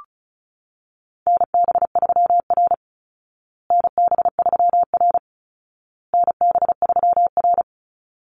Morse code, heard as a keyed tone, about 34 wpm, 700 Hz. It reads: NB3R NB3R NB3R
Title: Call Signs Repeated 2x - 35wpm